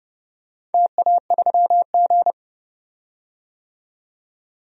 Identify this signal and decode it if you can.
Morse code, heard as a keyed tone, about 30 wpm, 700 Hz. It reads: TA3G
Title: DXCC-100 Call Signs - 4 Characters - 30wpm